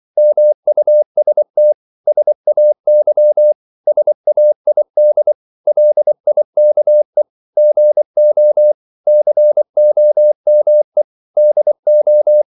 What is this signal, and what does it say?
Morse code, heard as a keyed tone, about 24 wpm, 600 Hz. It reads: MUST SAY SAID LIKE GO COME DO
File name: CW / Practice / 500MostCommonEnglishWords24